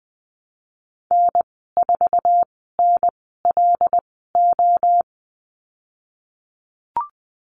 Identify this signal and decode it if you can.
Morse code, heard as a keyed tone, about 20 wpm, spaced as for 15 wpm, 700 Hz. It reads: N4NLO
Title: 128 Lesson 21g - Number 6 - Call Signs 20wpm